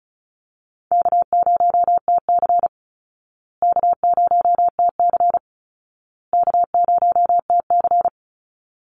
Morse code, heard as a keyed tone, about 34 wpm, 700 Hz. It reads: K0TC K0TC K0TC
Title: Easy Call Signs Repeated 3x - 35wpm